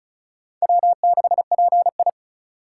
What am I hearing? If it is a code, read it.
Morse code, heard as a keyed tone, about 36 wpm, 700 Hz. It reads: W6PI
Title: USA Call Signs - 4 Characters - 4x Character Spacing - 35wpm